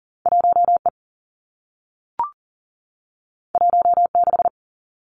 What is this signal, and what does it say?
Morse code, heard as a keyed tone, about 42 wpm, 700 Hz. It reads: 1 E 1 6
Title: Two Challenging Letter-Number Combinations 40wpm